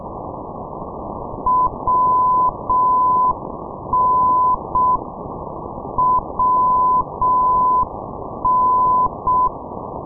Morse code, heard as a keyed tone, about 6 wpm, 1010 Hz, with noise. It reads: WN WN
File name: ndb minsk - machulishchi wn-458, sdr kiwi minsk